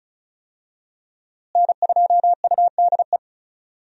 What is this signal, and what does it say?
Morse code, heard as a keyed tone, about 34 wpm, 700 Hz. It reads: N2UDE
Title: USA Call Signs - 5 Characters - Repeated 3x - 35wpm